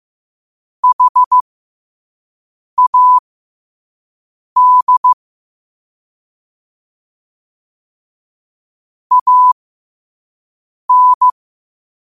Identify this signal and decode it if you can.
Morse code, heard as a keyed tone, about 15 wpm, spaced as for 5 wpm, 1000 Hz. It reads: HAD AN
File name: qotdfiles / 5wpm